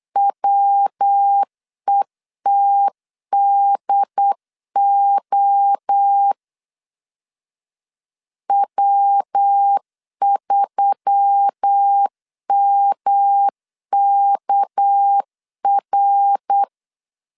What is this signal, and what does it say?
Morse code, heard as a keyed tone, about 8 wpm, 800 Hz. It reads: WETDO W3MKR
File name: CW / DET / 1 6 1500C S B 8wpm 5142